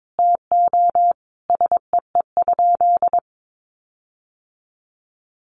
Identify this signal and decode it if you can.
Morse code, heard as a keyed tone, about 22 wpm, 700 Hz. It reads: TO SEE?
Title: Sentences from Top 1300 Words 22wpm